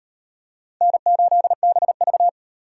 Morse code, heard as a keyed tone, about 35 wpm, 700 Hz. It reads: N8BV